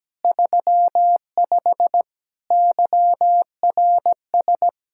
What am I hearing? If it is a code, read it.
Morse code, heard as a keyed tone, about 17 wpm, 700 Hz. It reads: 35 YRS
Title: Practice QSO - Exchange 3 - More Info - 17wpm